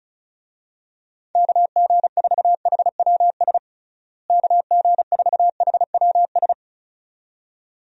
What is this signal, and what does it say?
Morse code, heard as a keyed tone, about 34 wpm, 700 Hz. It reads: KG4HWS KG4HWS
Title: USA Call Signs - 6 Characters - Repeated 2x - 35wpm